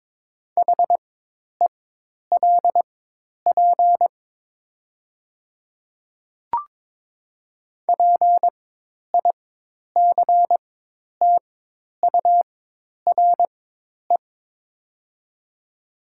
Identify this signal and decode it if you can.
Morse code, heard as a keyed tone, about 23 wpm, spaced as for 10 wpm, 700 Hz. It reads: HELP PICTURE
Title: 4x Character Spacing - Top 200 Words - 22wpm